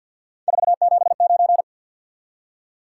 Morse code, heard as a keyed tone, about 48 wpm, 700 Hz. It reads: 479
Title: Top 180 QSO Elements 50wpm